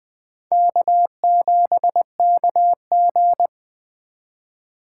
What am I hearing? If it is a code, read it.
Morse code, heard as a keyed tone, about 21 wpm, 700 Hz. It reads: K7KG
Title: USA Call Signs - 4x Character Spacing - 20wpm